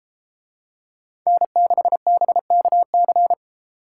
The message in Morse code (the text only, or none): N6BKC